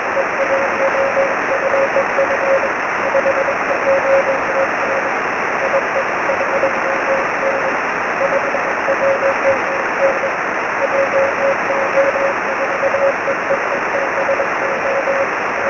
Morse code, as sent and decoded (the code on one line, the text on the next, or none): none